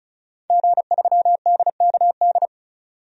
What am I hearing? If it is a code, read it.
Morse code, heard as a keyed tone, about 34 wpm, 700 Hz. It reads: G3DKD
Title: DXCC-100 Call Signs - 5 Characters - 35wpm